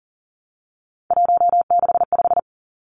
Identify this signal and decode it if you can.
Morse code, heard as a keyed tone, about 42 wpm, 690 Hz. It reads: 165
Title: Three Challenging Letter-Number Combinations 40wpm